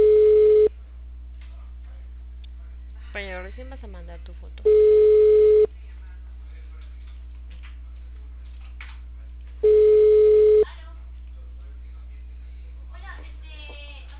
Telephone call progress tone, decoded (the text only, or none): ringback tone